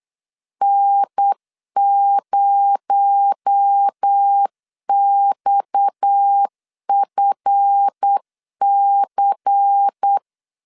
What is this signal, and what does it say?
Morse code, heard as a keyed tone, about 8 wpm, 800 Hz. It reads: N0XFC
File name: LEZIONI / 7 GF9Z / 7 GF9 8wpm 3545